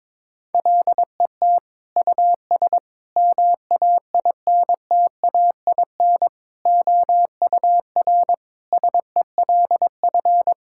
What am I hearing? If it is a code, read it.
Morse code, heard as a keyed tone, about 22 wpm, 700 Hz. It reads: LET US MAINTAIN OUR SELF
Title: Sentences from Top 1000 Words - Review Part 5 - 22wpm